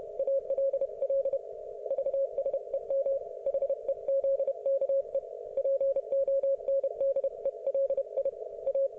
Morse code, heard as a keyed tone, about 31 wpm, 530 Hz. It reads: ALL VSEM HEZKE PONDELI A